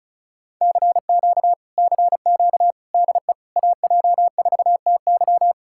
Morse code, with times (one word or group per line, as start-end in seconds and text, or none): 0.6-1.5 CQ
1.8-2.7 CQ
2.9-3.3 DE
3.6-5.5 AJ4TY